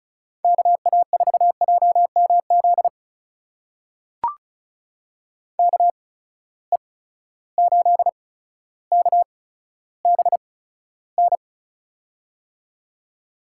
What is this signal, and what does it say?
Morse code, heard as a keyed tone, about 36 wpm, spaced as for 15 wpm, 700 Hz. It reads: KA4JMZ KE8KBN
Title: USA Call Signs - 6 Characters - 8x Character Spacing - 35wpm